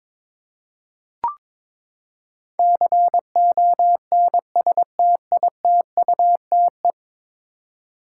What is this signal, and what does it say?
Morse code, heard as a keyed tone, about 21 wpm, 700 Hz. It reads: CONSTITUTE
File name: practice / 22wpm / Ten Letter Words - 22wpm